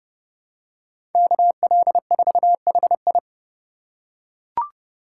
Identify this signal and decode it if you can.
Morse code, heard as a keyed tone, about 30 wpm, 700 Hz. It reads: KL4HI
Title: USA Call Signs - 5 Characters - 2x Character Spacing - 30wpm